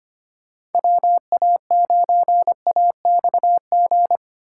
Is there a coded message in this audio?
Yes: Morse code from 0.7 to 4.2 s, heard as a keyed tone, about 32 wpm, 700 Hz. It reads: WA9AXG